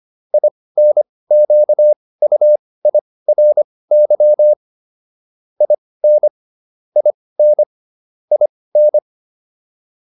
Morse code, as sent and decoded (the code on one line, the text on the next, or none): .. -. --.- ..- .. .-. -.-- / .. -. / .. -. / .. -.
INQUIRY IN IN IN